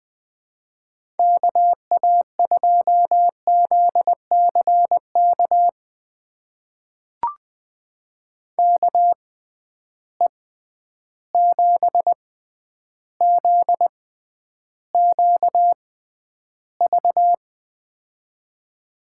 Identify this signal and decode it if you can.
Morse code, heard as a keyed tone, about 20 wpm, spaced as for 10 wpm, 700 Hz. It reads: KA2ZCK KE7ZQV
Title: USA Call Signs - 6 Characters - 6x Character Spacing - 20wpm